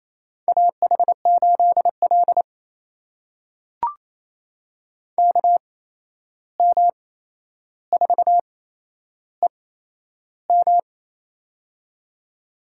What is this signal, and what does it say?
Morse code, heard as a keyed tone, about 28 wpm, spaced as for 11 wpm, 700 Hz. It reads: AH8L KM4EM
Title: USA Call Signs - 8x Character Spacing - 28wpm